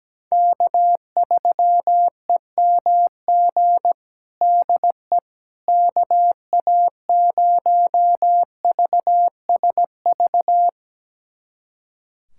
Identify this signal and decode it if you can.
Morse code, heard as a keyed tone, about 17 wpm, 700 Hz. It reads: K3EMG DE KA0VSV
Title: USA Callsign DE Callsign 17wpm